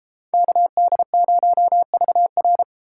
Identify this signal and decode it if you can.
Morse code, heard as a keyed tone, about 32 wpm, 700 Hz. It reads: KD0VR